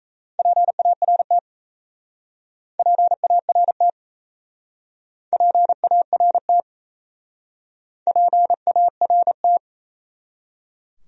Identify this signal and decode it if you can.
Morse code, heard as a keyed tone, about 37 wpm, 700 Hz. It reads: PART PART PART PART